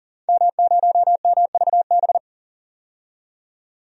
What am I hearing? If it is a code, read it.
Morse code, heard as a keyed tone, about 40 wpm, 700 Hz. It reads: M0MVB
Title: DXCC-100 Call Signs - 5 Characters - 40wpm